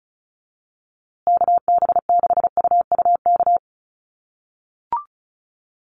Morse code, heard as a keyed tone, about 34 wpm, 700 Hz. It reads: KB6UUK
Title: USA Call Signs Repeated 3x 35wpm